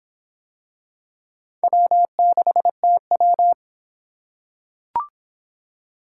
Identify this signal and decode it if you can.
Morse code, heard as a keyed tone, about 26 wpm, 700 Hz. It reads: W6TW